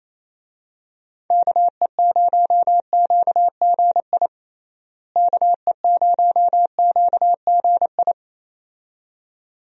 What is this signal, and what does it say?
Morse code, heard as a keyed tone, about 28 wpm, 700 Hz. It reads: KE0QGI KE0QGI
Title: USA Call Signs - 6 Characters - Repeated 2x - 28wpm